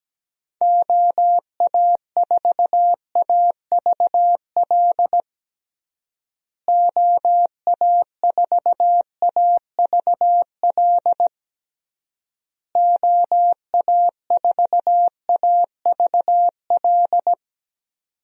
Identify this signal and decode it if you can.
Morse code, heard as a keyed tone, about 17 wpm, 700 Hz. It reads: OA4AVL OA4AVL OA4AVL
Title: USA Call Signs - 6 Characters - Repeated 3x - 17wpm